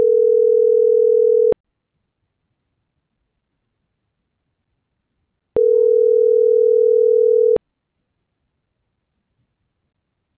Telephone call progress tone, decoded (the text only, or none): ringback tone